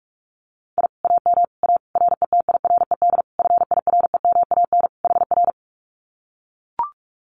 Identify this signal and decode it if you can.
Morse code, heard as a keyed tone, about 44 wpm, 700 Hz. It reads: I AM A RETIRED FIREMAN HR
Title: Common QSO Phrases Encore 45wpm